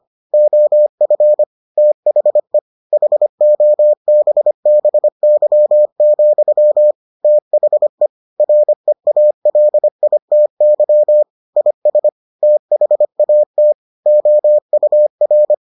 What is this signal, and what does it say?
Morse code, heard as a keyed tone, about 25 wpm, 600 Hz. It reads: OF THE HOBBY, THE REALITY IS THAT OUR